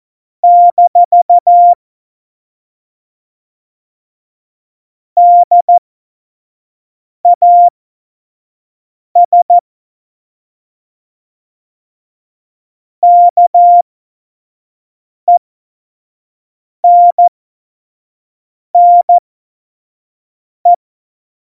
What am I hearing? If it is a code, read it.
Morse code, heard as a keyed tone, about 14 wpm, spaced as for 5 wpm, 700 Hz. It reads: - DAS KENNE
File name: Morsekurs / Witze / Witz 06 March 2018, 25 F70